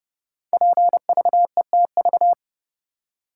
Morse code, heard as a keyed tone, about 30 wpm, 700 Hz. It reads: PVETV
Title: Code Groups - 5 Letters - 30wpm